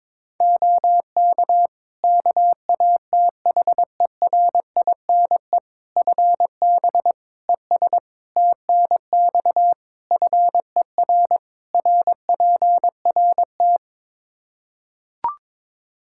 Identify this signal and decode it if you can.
Morse code, heard as a keyed tone, about 21 wpm, 700 Hz. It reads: OK KATHERINE FB ES TNX FER RPRT
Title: K1USN SST QSO Practice 22wpm